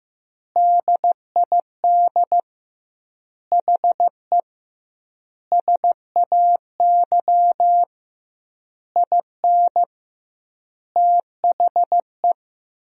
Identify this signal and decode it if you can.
Morse code, heard as a keyed tone, about 16 wpm, 700 Hz. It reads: DID HE SAY IN THE
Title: 2x Word-Spacing - Sentences from Top 300 Words - 15wpm